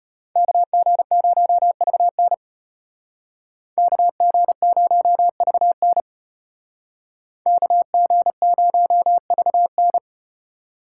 Morse code, heard as a keyed tone, about 34 wpm, 700 Hz. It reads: KG0VN KG0VN KG0VN